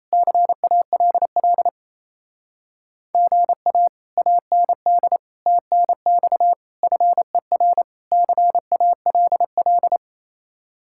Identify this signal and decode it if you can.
Morse code, heard as a keyed tone, about 34 wpm, 700 Hz. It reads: CALL GA AND TNX FER CALL
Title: Speed-Racing Common QSO Phrases Encore - 28wpm